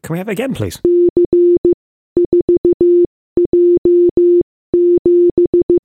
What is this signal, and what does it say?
Morse code, heard as a keyed tone, about 15 wpm, spaced as for 13 wpm, 350 Hz, with speech. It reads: C4J7